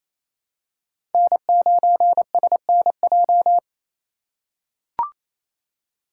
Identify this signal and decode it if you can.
Morse code, heard as a keyed tone, about 27 wpm, 700 Hz. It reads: N9SNJ
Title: USA Call Signs - 5 Characters - 4x Character Spacing - 28wpm